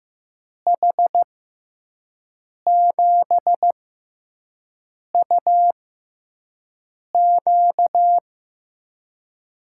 Morse code, heard as a keyed tone, about 15 wpm, spaced as for 5 wpm, 700 Hz. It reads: H7UQ
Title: USA Call Signs - 6x Character Spacing - 15wpm